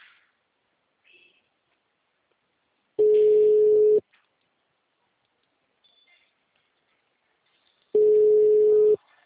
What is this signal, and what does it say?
Ringback tone (Europe, 425 Hz).